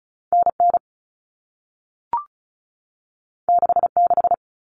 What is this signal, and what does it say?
Morse code, heard as a keyed tone, about 36 wpm, 700 Hz. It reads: NN 66
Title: Common Cut Numbers - 2-Digit 35wpm